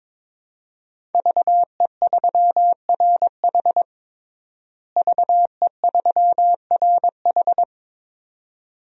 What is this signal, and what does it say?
Morse code, heard as a keyed tone, about 22 wpm, 700 Hz. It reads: VE3RH VE3RH
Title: DXCC-100 Call Signs - 5 Characters - Repeated 2x - 22wpm